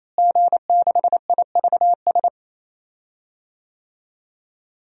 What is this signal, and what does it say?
Morse code, heard as a keyed tone, about 27 wpm, 700 Hz. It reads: G6IVS
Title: DXCC-100 Call Signs - 5 Characters - Repeated 3x - 28wpm